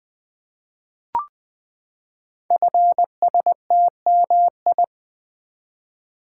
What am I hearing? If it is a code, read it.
Morse code, heard as a keyed tone, about 21 wpm, 700 Hz. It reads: FSTMI
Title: Code Groups - 5 Letters - 20wpm